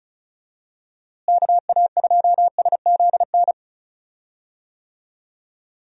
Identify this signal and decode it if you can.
Morse code, heard as a keyed tone, about 36 wpm, 700 Hz. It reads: KA2SZN
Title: USA Call Signs - 6 Characters - Repeated 3x - 35wpm